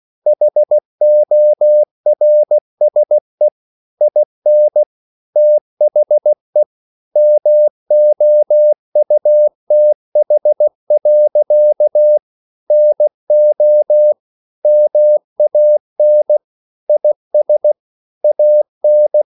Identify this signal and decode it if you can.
Morse code, heard as a keyed tone, about 16 wpm, 600 Hz. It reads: HORSE IN THE MOUTH. NO MAN IS AN
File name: CW/Practice